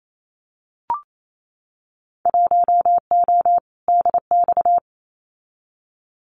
Morse code, heard as a keyed tone, about 28 wpm, 700 Hz. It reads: 1O DX
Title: Winter Field Day Exchanges Encore 28wpm